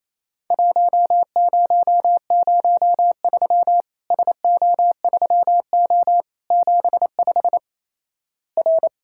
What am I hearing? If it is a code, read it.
Morse code, heard as a keyed tone, about 28 wpm, 700 Hz. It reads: 1003 SO3O 75 R
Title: Worked All Europe DX Contest - QTC Exchange - 28wpm